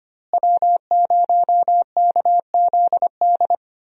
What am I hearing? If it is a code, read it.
Morse code, heard as a keyed tone, about 25 wpm, 690 Hz. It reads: W0KZD
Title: USA Call Signs - 5 Characters - Repeated 2x - 25wpm